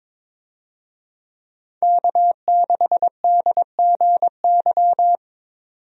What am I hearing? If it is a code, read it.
Morse code, heard as a keyed tone, about 22 wpm, 700 Hz. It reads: K6DGY
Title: USA Call Signs - 5 Characters - 22wpm